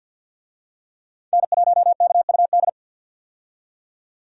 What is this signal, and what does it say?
Morse code, heard as a keyed tone, about 53 wpm, 700 Hz. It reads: N1KUD
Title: USA Call Signs - 5 Characters - 50wpm